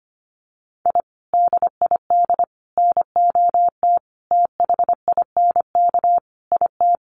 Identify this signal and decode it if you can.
Morse code, heard as a keyed tone, about 25 wpm, 700 Hz. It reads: I DID NOT THINK IT
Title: Sets of 5 Words Encore2 25wpm